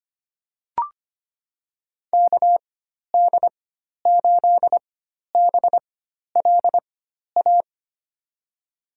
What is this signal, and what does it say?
Morse code, heard as a keyed tone, about 25 wpm, spaced as for 12 wpm, 700 Hz. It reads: KD8BLA
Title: USA Call Signs - 6 Characters - 4x Character Spacing - 25wpm